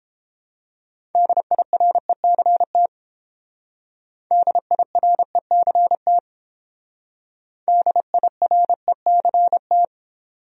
Morse code, heard as a keyed tone, about 28 wpm, 700 Hz. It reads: DIRECT DIRECT DIRECT